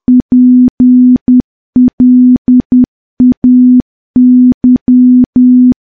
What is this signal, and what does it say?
Morse code, heard as a keyed tone, about 10 wpm, 250 Hz. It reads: PLAY
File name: morse / morse-training-10WPM